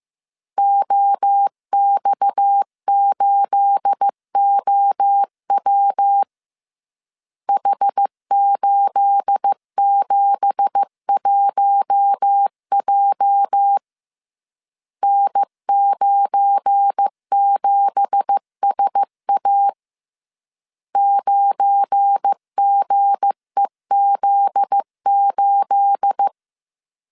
Morse code, heard as a keyed tone, about 15 wpm, 800 Hz. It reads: OX8OW H871J N97SA 9GEZ8
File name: CW / DET / 1 9 1500C S C 14wpm 3245